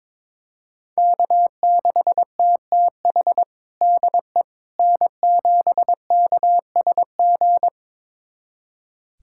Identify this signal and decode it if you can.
Morse code, heard as a keyed tone, about 22 wpm, 700 Hz. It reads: K6TTH DE N7KSG
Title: USA Callsign DE Callsign - 5 Characters - 22wpm